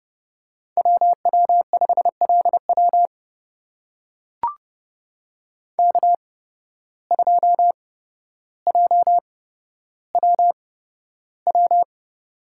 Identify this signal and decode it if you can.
Morse code, heard as a keyed tone, about 31 wpm, spaced as for 13 wpm, 700 Hz. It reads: WW5LW K2JWW
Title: USA Call Signs - 5 Characters - 8x Character Spacing - 30wpm